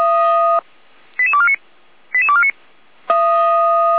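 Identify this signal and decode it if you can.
Five-tone selective call (ZVEI), heard as quick runs of five tones, tones of about 70 ms. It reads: 88249 88249
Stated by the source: ZVEI five-tone call received on SDR, from feuerwehr-orlamuende fire brigade audio server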